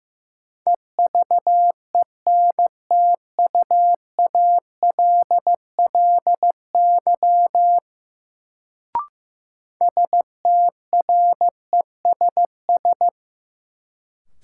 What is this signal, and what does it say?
Morse code, heard as a keyed tone, about 15 wpm, 700 Hz. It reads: EVENTUALLY STRESS
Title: Sentences from Top 1000 Words - Review Part 6 - New Word Drill - 15wpm